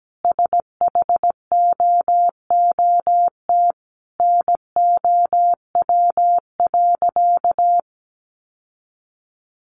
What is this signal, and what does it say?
Morse code, heard as a keyed tone, about 17 wpm, 700 Hz. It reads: SHOOT NOW.
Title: Sentences from Top 1200 Words 17wpm